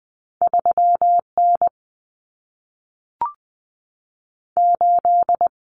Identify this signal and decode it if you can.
Morse code, heard as a keyed tone, about 20 wpm, 700 Hz. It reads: 3N 8
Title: Common Cut Numbers - 2-Digit 20wpm